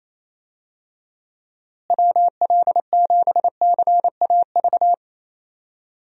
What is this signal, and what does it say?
Morse code, heard as a keyed tone, about 28 wpm, 700 Hz. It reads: WL7CAV
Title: USA Call Signs - 6 Characters - Repeated 3x - 28wpm